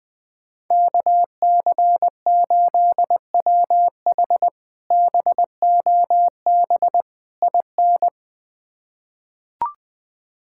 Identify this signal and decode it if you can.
Morse code, heard as a keyed tone, about 20 wpm, 700 Hz. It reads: KC8WH BOB IN
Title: K1USN SST QSO Practice - 20wpm